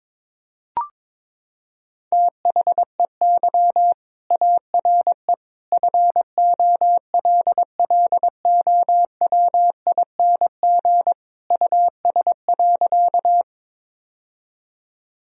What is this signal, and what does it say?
Morse code, heard as a keyed tone, about 21 wpm, 700 Hz. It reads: THEY ARE FOLLOWING US.
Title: Sentences from Top 1000 Words - Review Part 4 - 22wpm